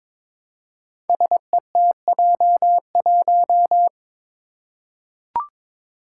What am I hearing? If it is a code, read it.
Morse code, heard as a keyed tone, about 22 wpm, 700 Hz. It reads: SETJ1
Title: Five Challenging Letter-Number Combinations 22wpm